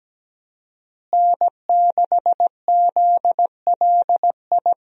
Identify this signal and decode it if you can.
Morse code, heard as a keyed tone, about 17 wpm, 700 Hz. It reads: N6ZLI
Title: USA Call Signs - 5 Characters - Repeated 2x - 17wpm